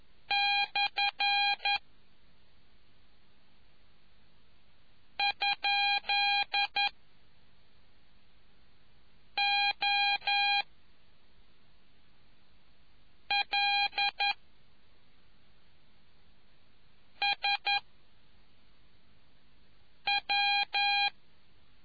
Morse code, heard as a keyed tone, about 11 wpm, 2450 Hz. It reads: / ? O L S W